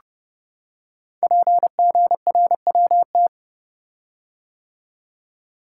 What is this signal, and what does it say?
Morse code, heard as a keyed tone, about 30 wpm, 700 Hz. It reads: PGRWT
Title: Code Groups - 5 Letters - 30wpm